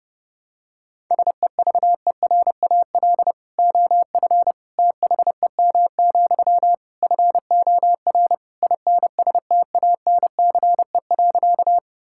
Morse code, heard as a keyed tone, about 30 wpm, 700 Hz. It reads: SEVERAL OF THEM, FOR INSTANCE.
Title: Sentences from Top 1000 Words - Review Part 5 - 30wpm